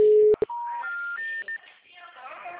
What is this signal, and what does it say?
Special information tone, with speech.